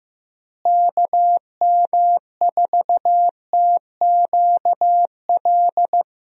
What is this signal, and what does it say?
Morse code, heard as a keyed tone, about 15 wpm, 700 Hz. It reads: KM4TQL
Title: USA Call Signs - 6 Characters - Repeated 3x - 15wpm